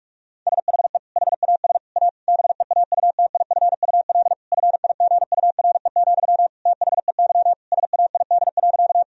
Morse code, heard as a keyed tone, about 46 wpm, 700 Hz. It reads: SHE HAS A BEAUTIFUL FIGURE, THEY SAID.
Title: Sentences from Top 1000 Words - Review Part 2 - 45wpm